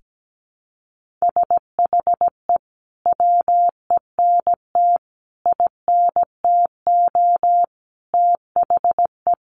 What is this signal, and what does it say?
Morse code, heard as a keyed tone, about 17 wpm, 700 Hz. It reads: SHE WENT INTO THE
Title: Sentences from Top 1400 Words 17wpm